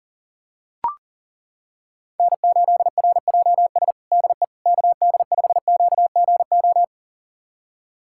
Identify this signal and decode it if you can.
Morse code, heard as a keyed tone, about 37 wpm, 700 Hz. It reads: N8RJS DE KD5QGO
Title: USA Callsign DE Callsign 40wpm